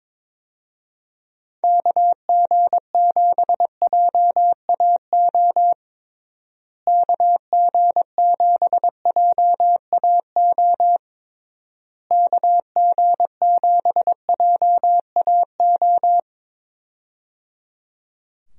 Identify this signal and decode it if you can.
Morse code, heard as a keyed tone, about 22 wpm, 700 Hz. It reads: KG7JAO KG7JAO KG7JAO
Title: USA Call Signs - 6 Characters - Repeated 3x - 22wpm